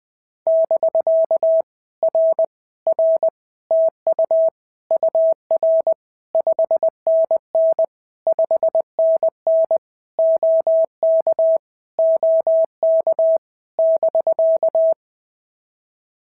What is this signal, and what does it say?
Morse code, heard as a keyed tone, about 19 wpm, 650 Hz. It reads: <BK> R R TU UR 5NN 5NN OK OK <BK>
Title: POTA QSO Practice - 20wpm